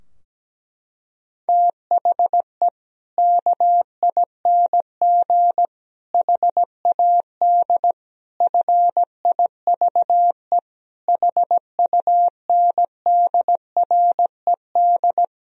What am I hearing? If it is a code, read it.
Morse code, heard as a keyed tone, about 17 wpm, 700 Hz. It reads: THE KING HAD FIVE HUNDRED
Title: Sentences from Top 1000 Words - Review Part 2 - 17wpm